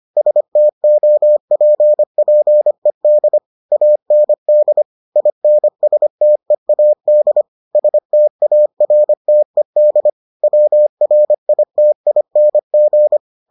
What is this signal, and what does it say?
Morse code, heard as a keyed tone, about 25 wpm, 600 Hz. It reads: STOPPED AND INSTEAD STARTED WRITING